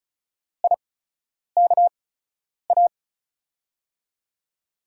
Morse code, heard as a keyed tone, about 34 wpm, spaced as for 10 wpm, 700 Hz. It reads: IKA
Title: USA Call Signs - 5 Characters - 8x Character Spacing - 35wpm